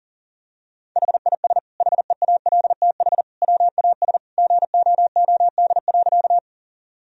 Morse code, heard as a keyed tone, about 40 wpm, 700 Hz. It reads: HIS HEALTH WAS GOOD.
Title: Sentences from Top 1000 Words - Review Part 3 - 40wpm